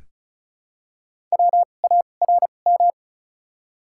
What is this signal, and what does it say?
Morse code, heard as a keyed tone, about 34 wpm, 700 Hz. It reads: W A R M
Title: 2x Character Spacing - Top 500 Words - 35wpm